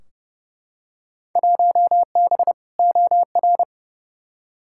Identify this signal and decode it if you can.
Morse code, heard as a keyed tone, about 30 wpm, 700 Hz. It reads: 1B OR
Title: Field Day Exchanges - 30wpm